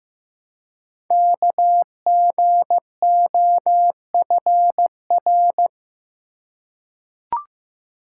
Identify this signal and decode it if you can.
Morse code, heard as a keyed tone, about 15 wpm, 700 Hz. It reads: KGOFR
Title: Code Groups - 5 Letters - 15wpm